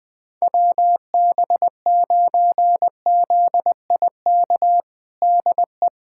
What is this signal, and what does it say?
Morse code, heard as a keyed tone, about 21 wpm, 700 Hz. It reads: WB9ZIK DE
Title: USA Callsign DE Callsign - 6 Characters - 20wpm